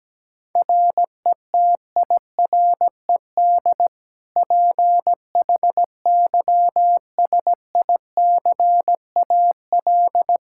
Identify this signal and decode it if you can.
Morse code, heard as a keyed tone, about 17 wpm, 700 Hz. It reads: RETIRED PHYSICAL
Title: Practice QSO - Exchange 3 - More Info - 17wpm